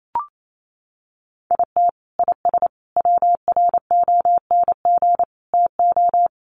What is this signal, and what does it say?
Morse code, heard as a keyed tone, about 28 wpm, 700 Hz. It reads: IT IS WRONG TO
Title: Sentences from Top 1300 Words 28wpm